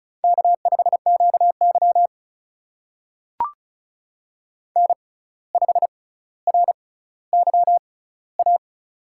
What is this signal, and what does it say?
Morse code, heard as a keyed tone, about 36 wpm, spaced as for 17 wpm, 700 Hz. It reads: K5QY N5RYA
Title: USA Call Signs - 6x Character Spacing - 35wpm